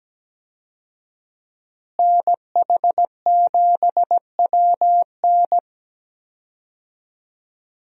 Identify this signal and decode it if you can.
Morse code, heard as a keyed tone, about 17 wpm, 700 Hz. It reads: NH7WN
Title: USA Call Signs - 5 Characters - Repeated 2x - 17wpm